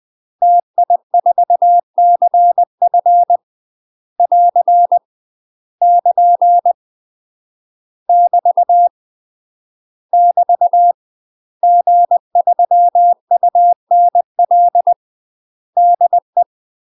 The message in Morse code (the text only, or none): TI4CF <AR> <KN> <BT> <BT> G3UNL DE